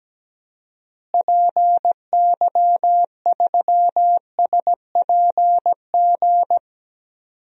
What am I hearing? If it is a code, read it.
Morse code, heard as a keyed tone, about 17 wpm, 700 Hz. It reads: PY3SPG
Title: DXCC-100 Call Signs - 6 Characters - Repeated 2x - 17wpm